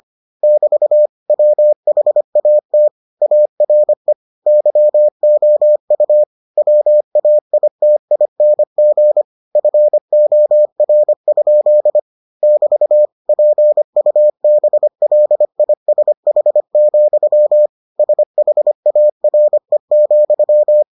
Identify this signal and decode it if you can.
Morse code, heard as a keyed tone, about 25 wpm, 600 Hz. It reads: <BT> WHAT ARE YOU WAITING FOR? <BT> PUBLISH, SHARE,